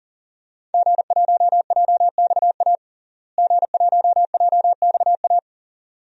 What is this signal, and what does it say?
Morse code, heard as a keyed tone, about 40 wpm, 700 Hz. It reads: G1JXA G1JXA
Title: DXCC-100 Call Signs - 5 Characters - Repeated 2x - 40wpm